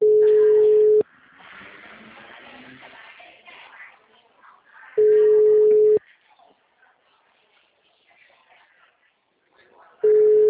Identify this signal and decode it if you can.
Ringback tone (Europe, 425 Hz).